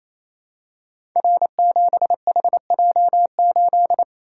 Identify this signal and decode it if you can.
Morse code, heard as a keyed tone, about 27 wpm, 700 Hz. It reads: R7HJ8
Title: Code Groups - 5 Letter:Numbers - 28wpm